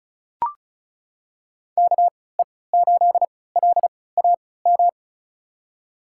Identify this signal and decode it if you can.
Morse code, heard as a keyed tone, about 36 wpm, spaced as for 20 wpm, 700 Hz. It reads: KE8LAM
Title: USA Call Signs - 6 Characters - 3x Character Spacing - 35wpm